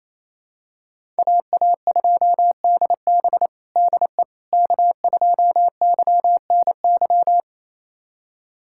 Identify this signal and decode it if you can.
Morse code, heard as a keyed tone, about 27 wpm, 700 Hz. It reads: AA2DB DE K2YNY
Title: USA Callsign DE Callsign - 5 Characters - 28wpm